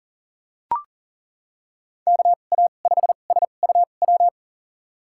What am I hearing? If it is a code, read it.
Morse code, heard as a keyed tone, about 39 wpm, spaced as for 29 wpm, 700 Hz. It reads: KA5SUW
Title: USA Call Signs - 6 Characters - 2x Character Spacing - 40wpm